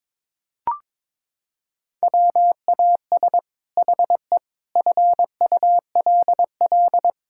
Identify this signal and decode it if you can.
Morse code, heard as a keyed tone, about 22 wpm, 700 Hz. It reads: WAS HE FULL
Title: Sentences from Top 1000 Words - Review Part 6 - 22wpm